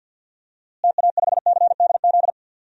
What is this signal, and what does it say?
Morse code, heard as a keyed tone, about 52 wpm, 700 Hz. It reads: TA5CDZ
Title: DXCC-100 Call Signs - 6 Characters - Repeated 2x - 50wpm